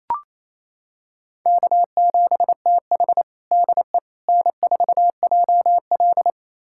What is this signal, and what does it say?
Morse code, heard as a keyed tone, about 28 wpm, 700 Hz. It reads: K7TH DE N4JL
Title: USA Callsign DE Callsign - 4 Characters - 28wpm